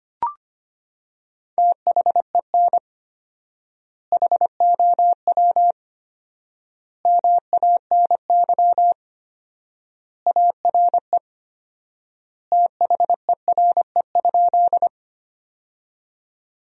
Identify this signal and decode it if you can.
Morse code, heard as a keyed tone, about 25 wpm, 700 Hz. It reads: THEN HOW MANY ARE THERE?
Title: 4x Word-Spacing - Sentences from Top 100 Words - 25wpm